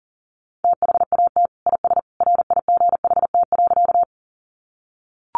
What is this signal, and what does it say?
Morse code, heard as a keyed tone, about 40 wpm, 700 Hz. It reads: THAT IS RIGHT.
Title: Sentences from Top 200 Words 40wpm